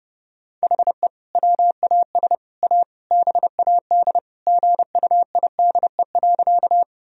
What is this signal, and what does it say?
Morse code, heard as a keyed tone, about 29 wpm, 700 Hz. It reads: HE WAS A BAD GUIDE.
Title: Sentences from Top 1000 Words - Review Part 3 - 30wpm